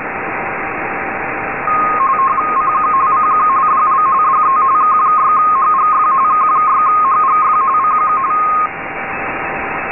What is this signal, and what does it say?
RTTY, about 45.45 baud. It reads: CQ CQ DL DE K6HGF K6HGF CQ
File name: rtty-D2000Hz